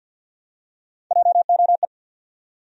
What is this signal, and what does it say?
Morse code, heard as a keyed tone, about 47 wpm, 700 Hz. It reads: JOE
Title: USA Common Names - Max 5 Letters - 50wpm